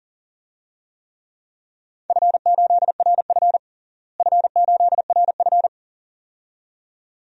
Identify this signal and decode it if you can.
Morse code, heard as a keyed tone, about 40 wpm, 700 Hz. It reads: F8RF F8RF
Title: DXCC-100 Call Signs - 4 Characters - Repeated 2x - 40wpm